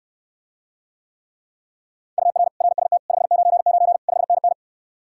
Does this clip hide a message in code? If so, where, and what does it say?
Yes: Morse code from 2.2 to 4.5 s, heard as a keyed tone, about 69 wpm, 700 Hz. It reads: UR RST 599 5NN